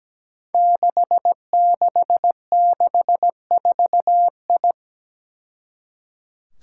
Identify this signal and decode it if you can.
Morse code, heard as a keyed tone, about 17 wpm, 700 Hz. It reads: 6664I
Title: Five Challenging Letter-Number Combinations 17wpm